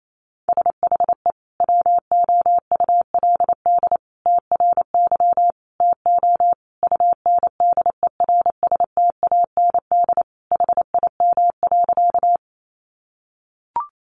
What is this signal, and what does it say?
Morse code, heard as a keyed tone, about 28 wpm, 700 Hz. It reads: SHE WOULD TRY TO UNDERSTAND HIM.
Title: Sentences from Top 600 Words 28wpm